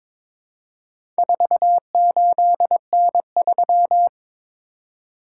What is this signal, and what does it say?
Morse code, heard as a keyed tone, about 22 wpm, 700 Hz. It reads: 48N3
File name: practice / 22wpm / Four Letter-Number Combinations - 22wpm